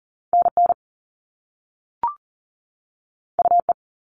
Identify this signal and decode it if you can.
Morse code, heard as a keyed tone, about 38 wpm, 700 Hz. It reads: NN UE
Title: All Cut Numbers - 2-Digit 40wpm